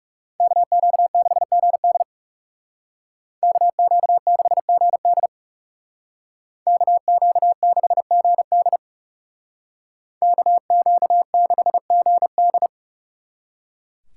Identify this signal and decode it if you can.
Morse code, heard as a keyed tone, about 39 wpm, 700 Hz. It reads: KQ6GD KQ6GD KQ6GD KQ6GD